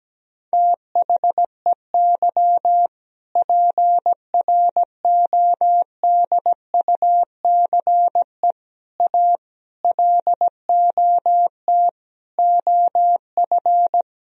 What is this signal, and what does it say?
Morse code, heard as a keyed tone, about 17 wpm, 700 Hz. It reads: THEY PRODUCE A LOT OF
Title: Sentences from Top 1000 Words - Review Part 3 - 17wpm